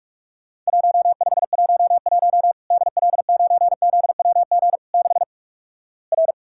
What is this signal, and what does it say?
Morse code, heard as a keyed tone, about 46 wpm, 700 Hz. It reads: 1511 DL9ZWG 6 R
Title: Worked All Europe DX Contest - QTC Exchange - 45wpm